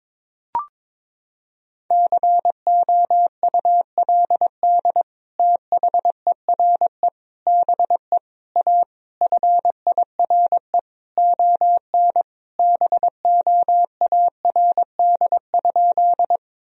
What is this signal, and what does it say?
Morse code, heard as a keyed tone, about 21 wpm, 700 Hz. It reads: COULD THERE BE A FIRE ON BOARD?
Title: Sentences from Top 1000 Words - Review Part 4 - 22wpm